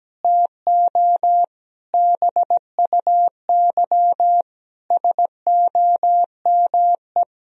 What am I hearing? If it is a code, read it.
Morse code, heard as a keyed tone, about 17 wpm, 700 Hz. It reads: TO BUY SOME
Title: Sentences from Top 1000 Words - Review Part 4 - 17wpm